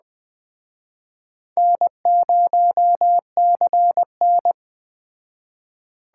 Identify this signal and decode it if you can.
Morse code, heard as a keyed tone, about 21 wpm, 700 Hz. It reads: N0CN
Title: Speed Racing - USA Call Signs 17wpm